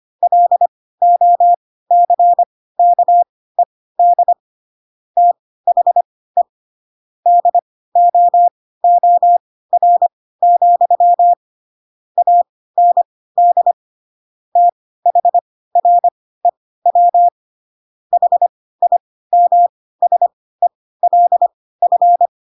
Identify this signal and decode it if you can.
Morse code, heard as a keyed tone, about 25 wpm, spaced as for 16 wpm, 700 Hz. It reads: LOCKED THE DOOR, AND THREW HIMSELF
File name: Pinocchio / Pinocchio ch4 10 25 ik1hge